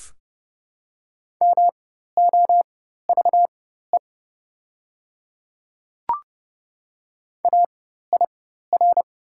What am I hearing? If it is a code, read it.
Morse code, heard as a keyed tone, about 30 wpm, spaced as for 14 wpm, 700 Hz. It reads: MOVE AIR
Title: 4x Character Spacing - Top 200 Words - 30wpm